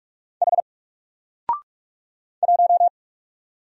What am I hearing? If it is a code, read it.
Morse code, heard as a keyed tone, about 46 wpm, 700 Hz. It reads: H 1
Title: Single Letter-Number Alt 45wpm